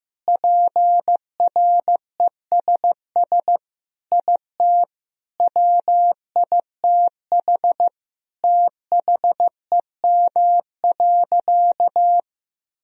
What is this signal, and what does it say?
Morse code, heard as a keyed tone, about 16 wpm, 700 Hz. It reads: PRESS IT WITH THEM.
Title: Sentences from Top 1000 Words - Review Part 1 - 15wpm